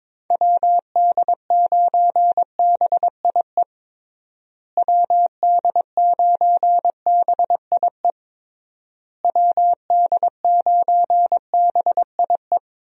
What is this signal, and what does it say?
Morse code, heard as a keyed tone, about 22 wpm, 700 Hz. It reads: WD9BIE WD9BIE WD9BIE
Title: USA Call Signs - 6 Characters - Repeated 3x - 22wpm